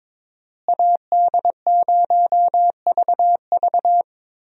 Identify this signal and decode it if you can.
Morse code, heard as a keyed tone, about 22 wpm, 700 Hz. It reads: AD0VV
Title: USA Call Signs - 5 Characters - Repeated 2x - 22wpm